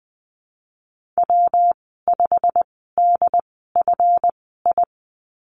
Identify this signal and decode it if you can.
Morse code, heard as a keyed tone, about 20 wpm, spaced as for 15 wpm, 700 Hz. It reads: W5DFI
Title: 154 Lesson 25g - Letter p - Call Signs 20wpm